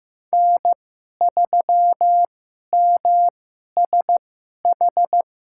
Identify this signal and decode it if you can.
Morse code, heard as a keyed tone, about 15 wpm, spaced as for 11 wpm, 700 Hz. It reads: N3MSH
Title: USA Call Signs - 5 Characters - 2x Character Spacing - 15wpm